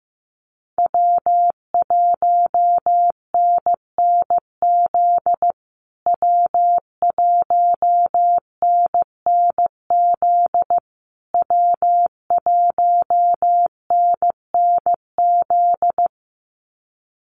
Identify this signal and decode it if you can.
Morse code, heard as a keyed tone, about 15 wpm, 700 Hz. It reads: W1NNZ W1NNZ W1NNZ
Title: USA Call Signs Repeated 3x 15wpm